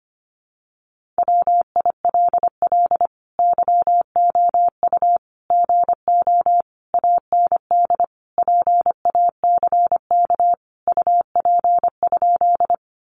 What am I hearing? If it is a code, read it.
Morse code, heard as a keyed tone, about 25 wpm, 700 Hz. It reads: WILL YOU GO AND PACK UP?
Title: Sentences from Top 1300 Words 25wpm